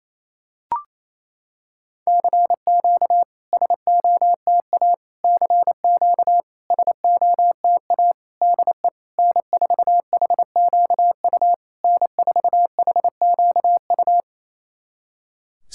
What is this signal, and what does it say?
Morse code, heard as a keyed tone, about 28 wpm, 700 Hz. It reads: CQ SOTA CQ SOTA DE N4HQU N4HQU
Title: SOTA QSO Practice 28wpm